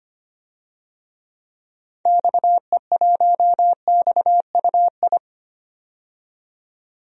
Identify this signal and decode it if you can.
Morse code, heard as a keyed tone, about 25 wpm, 700 Hz. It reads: XE1XUI
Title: DXCC-100 Call Signs - 6 Characters - 25wpm